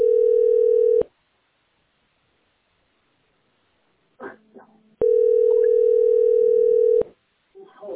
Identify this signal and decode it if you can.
Ringback tone (North America), 440 + 480 Hz.